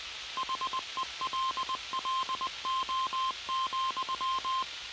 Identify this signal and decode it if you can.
Morse code, heard as a keyed tone, about 20 wpm, 1050 Hz, with noise. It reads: HELLO,